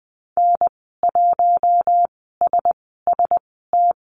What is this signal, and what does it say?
Morse code, heard as a keyed tone, about 20 wpm, spaced as for 15 wpm, 700 Hz. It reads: N1SST
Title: 042 Lesson 8g - Number 1 - Call Signs 20wpm